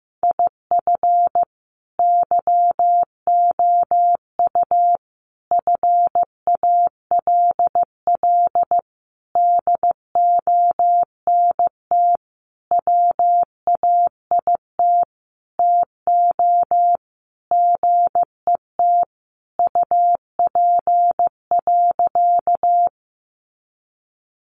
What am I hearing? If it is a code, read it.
Morse code, heard as a keyed tone, about 15 wpm, 700 Hz. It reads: IF YOU FALL DONT WAIT TO GET UP.
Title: Sentences from Top 500 Words 15wpm